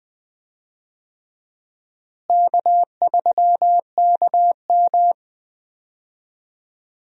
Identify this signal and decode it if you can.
Morse code, heard as a keyed tone, about 21 wpm, 700 Hz. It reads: K3KM
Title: USA Call Signs - 4 Characters - Repeated 3x - 20wpm